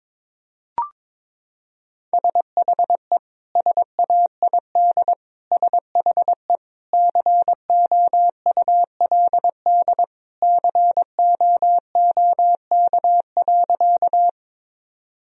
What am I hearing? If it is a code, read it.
Morse code, heard as a keyed tone, about 22 wpm, 700 Hz. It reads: SHE SAID SHE COULD COOK.
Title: Sentences from Top 1000 Words - Review Part 4 - 22wpm